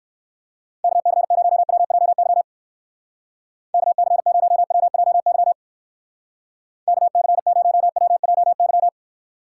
Morse code, heard as a keyed tone, about 59 wpm, 700 Hz. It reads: KC0WJY KC0WJY KC0WJY